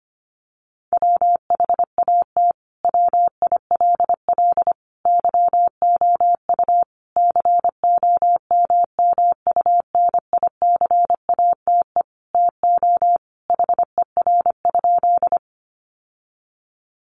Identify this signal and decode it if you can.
Morse code, heard as a keyed tone, about 25 wpm, 700 Hz. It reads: WHAT WILL YOU COMMUNICATE TO HER?
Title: Sentences from Top 1300 Words 25wpm